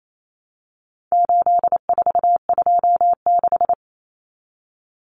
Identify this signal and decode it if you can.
Morse code, heard as a keyed tone, about 28 wpm, 700 Hz. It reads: 8426
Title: Numbers - 4-Digit 28wpm